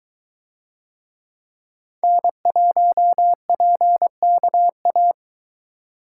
Morse code, heard as a keyed tone, about 22 wpm, 700 Hz. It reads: N1PKA